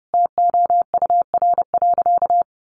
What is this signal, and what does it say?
Morse code, heard as a keyed tone, about 28 wpm, 700 Hz. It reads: TOUR.
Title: Sentences from Top 1400 Words 30wpm